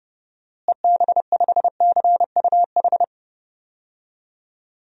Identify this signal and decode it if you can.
Morse code, heard as a keyed tone, about 32 wpm, 700 Hz. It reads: EB5CUH
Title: DXCC-100 Call Signs - 6 Characters - Repeated 3x - 30wpm